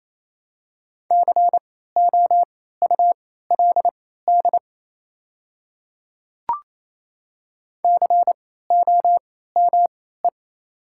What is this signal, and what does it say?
Morse code, heard as a keyed tone, about 28 wpm, spaced as for 16 wpm, 690 Hz. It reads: COULD COME
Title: 3x Character Spacing - Top 100 Words - 28wpm